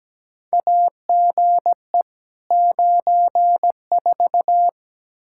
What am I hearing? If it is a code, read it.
Morse code, heard as a keyed tone, about 17 wpm, 700 Hz. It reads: AGE 94
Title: K1USN SST QSO Practice 17wpm